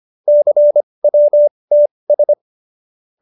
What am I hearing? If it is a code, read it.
Morse code, heard as a keyed tone, about 25 wpm, spaced as for 20 wpm, 590 Hz. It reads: CWTS
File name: QSO316 20